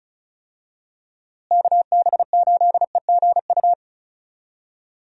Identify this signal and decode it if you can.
Morse code, heard as a keyed tone, about 36 wpm, 700 Hz. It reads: KB8EGU
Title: USA Call Signs - 6 Characters - Repeated 3x - 35wpm